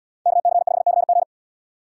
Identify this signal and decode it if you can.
Morse code, heard as a keyed tone, about 69 wpm, 690 Hz. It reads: KC5CG